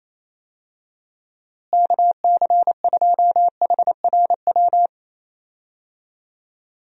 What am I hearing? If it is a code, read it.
Morse code, heard as a keyed tone, about 28 wpm, 700 Hz. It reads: KC2HRW
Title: USA Call Signs - 6 Characters - Repeated 3x - 28wpm